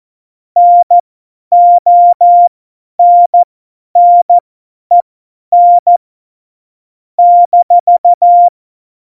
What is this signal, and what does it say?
Morse code, heard as a keyed tone, about 14 wpm, spaced as for 10 wpm, 700 Hz. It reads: NONNEN -
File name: Morsekurs / Witze / Witz 12. Januar 2021, 50 F70